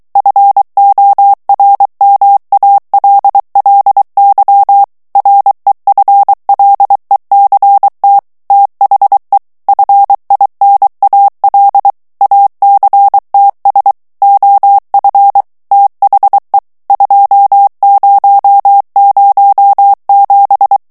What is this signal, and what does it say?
Morse code, heard as a keyed tone, about 23 wpm, 800 Hz. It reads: FORMALLY REFLECT THE FINAL ACTS OF THE 2007
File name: ANGLAIS texte clair 3 vitesse 22 mots minute